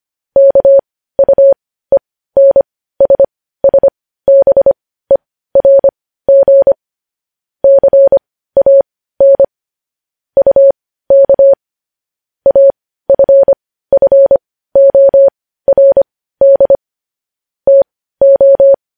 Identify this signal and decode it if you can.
Morse code, heard as a keyed tone, about 25 wpm, spaced as for 15 wpm, 550 Hz. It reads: KUENSSBERG CAN UK AFFORD TO
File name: podcast / News headlines-1x-25x15